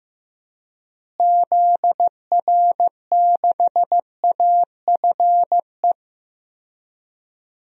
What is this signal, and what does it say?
Morse code, heard as a keyed tone, about 15 wpm, 700 Hz. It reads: ZR6AFE
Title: DXCC-100 Call Signs - 6 Characters - Repeated 2x - 15wpm